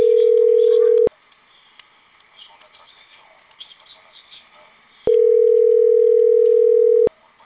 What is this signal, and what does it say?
Ringback tone (North America), 440 + 480 Hz.